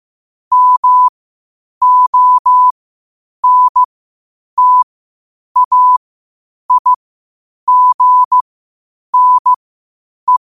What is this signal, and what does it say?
Morse code, heard as a keyed tone, about 15 wpm, spaced as for 8 wpm, 1000 Hz. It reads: MONTAIGNE
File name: qotdfiles / 8wpm